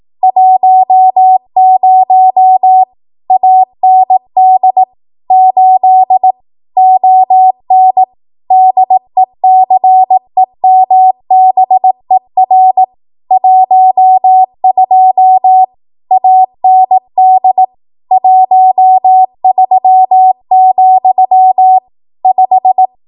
Morse code, heard as a keyed tone, about 18 wpm, 750 Hz. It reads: 10 AND 8 ON DECEMBER 12 AND 13, 5
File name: Archive / 18 WPM / 18WPM